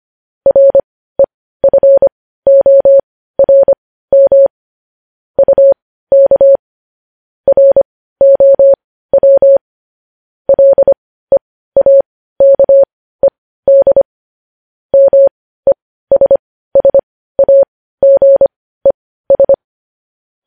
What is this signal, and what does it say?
Morse code, heard as a keyed tone, about 25 wpm, spaced as for 15 wpm, 550 Hz. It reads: REFORM UK ROW LEAKED MESSAGES
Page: news headlines in morse code at 15 wpm